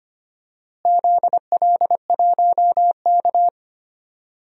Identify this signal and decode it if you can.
Morse code, heard as a keyed tone, about 25 wpm, 700 Hz. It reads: ZL1K
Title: DXCC-100 Call Signs - 4 Characters - Repeated 2x - 25wpm